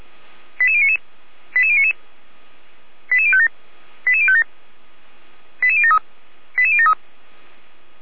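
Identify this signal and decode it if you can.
Five-tone selective call (ZVEI), heard as quick runs of five tones, tones of about 70 ms. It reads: 88099 88099 88057 88057 88073 88073
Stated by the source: ZVEI five-tone selective call received by SDR, from the Feuerwehr Orlamünde audio server.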